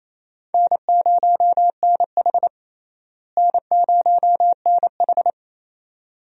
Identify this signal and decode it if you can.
Morse code, heard as a keyed tone, about 28 wpm, 700 Hz. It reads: N0NH N0NH
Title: USA Call Signs - 4 Characters - Repeated 2x - 28wpm